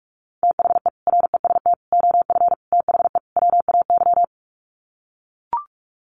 Morse code, heard as a keyed tone, about 44 wpm, 700 Hz. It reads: THE REST OF THE WAY
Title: Sets of 5 Words Encore 45wpm